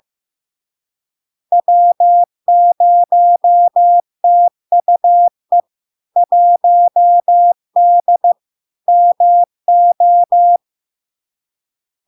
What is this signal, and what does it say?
Morse code, heard as a keyed tone, about 15 wpm, 700 Hz. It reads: W0TUE 1D MO
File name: LARK-FD-2022-Log-in-CW-15wpm-700-Hz